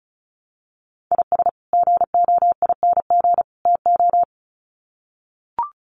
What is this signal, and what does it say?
Morse code, heard as a keyed tone, about 34 wpm, 700 Hz. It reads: IS GOING TO
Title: Sets of 3 Words 35wpm